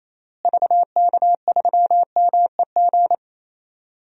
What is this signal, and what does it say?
Morse code, heard as a keyed tone, about 28 wpm, 700 Hz. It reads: VK3MEG
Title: DXCC-100 Call Signs - 6 Characters - Repeated 3x - 28wpm